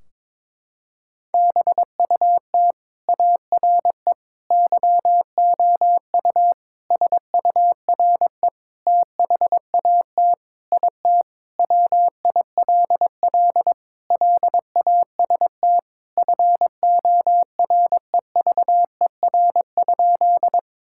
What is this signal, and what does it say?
Morse code, heard as a keyed tone, about 22 wpm, 700 Hz. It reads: BUT ARE YOU SURE THAT IT WILL LAST FOREVER?
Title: Sentences from Top 1600 Words 22wpm